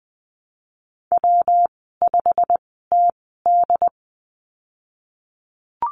Morse code, heard as a keyed tone, about 19 wpm, spaced as for 14 wpm, 700 Hz. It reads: W5TD
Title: 128 Lesson 21g - Number 6 - Call Signs 20wpm